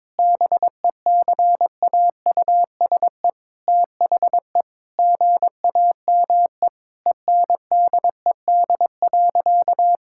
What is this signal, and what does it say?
Morse code, heard as a keyed tone, about 22 wpm, 700 Hz. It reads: BECAUSE THE GAME ENDED.
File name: practice / 22wpm / Sentences with 2-Word Phrases - 22wpm